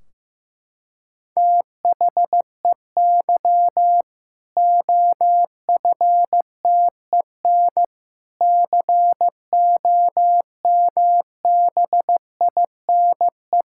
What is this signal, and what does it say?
Morse code, heard as a keyed tone, about 16 wpm, 700 Hz. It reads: THEY OFTEN COMBINE
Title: Sentences from Top 1600 Words 15wpm